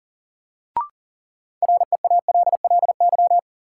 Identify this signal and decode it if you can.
Morse code, heard as a keyed tone, about 38 wpm, 700 Hz. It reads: REALLY
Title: Sets of 4 Words - New Word Drill - 40wpm